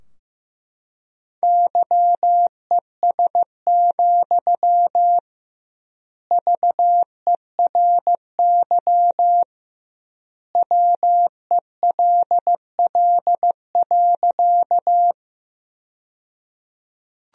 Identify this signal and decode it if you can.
Morse code, heard as a keyed tone, about 15 wpm, 700 Hz. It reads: YES, VERY WELL.
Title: 2x Word-Spacing - Sentences from Top 500 Words - 15wpm